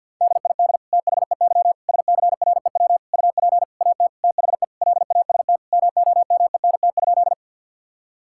Morse code, heard as a keyed tone, about 50 wpm, 700 Hz. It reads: DID THEY SCREW UP AT THE LAST MOMENT?
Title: Sentences from Top 1600 Words 50wpm